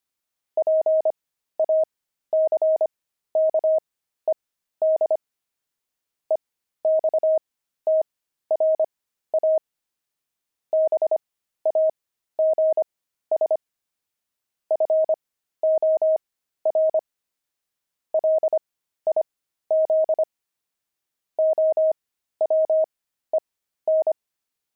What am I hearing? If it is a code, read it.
Morse code, heard as a keyed tone, about 25 wpm, spaced as for 13 wpm, 630 Hz. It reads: PACKED EXTRA BAGS FOR LIZ OWEN
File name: pangram-fs-13-cs-25-quit